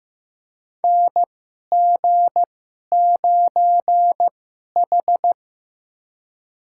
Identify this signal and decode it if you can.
Morse code, heard as a keyed tone, about 15 wpm, spaced as for 11 wpm, 700 Hz. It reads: NG9H
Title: USA Call Signs - 4 Characters - 2x Character Spacing - 15wpm